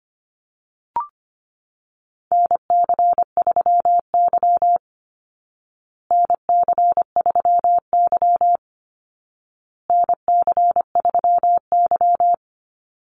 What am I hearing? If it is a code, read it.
Morse code, heard as a keyed tone, about 25 wpm, 700 Hz. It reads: NC3Y NC3Y NC3Y
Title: Call Signs Repeated 2x - 25wpm